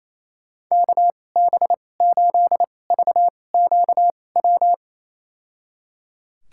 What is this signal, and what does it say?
Morse code, heard as a keyed tone, about 28 wpm, spaced as for 20 wpm, 700 Hz. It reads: KB8VQW
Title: USA Call Signs - 6 Characters - 2x Character Spacing - 28wpm